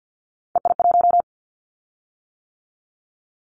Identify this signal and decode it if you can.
Morse code, heard as a keyed tone, about 50 wpm, 700 Hz. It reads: EI1
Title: Callsign Prefixes for Top 50 DXCC - 50wpm